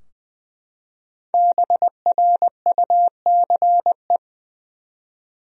Morse code, heard as a keyed tone, about 20 wpm, 700 Hz. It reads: BRUCE
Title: USA Common Names - Max 5 Letters - 20wpm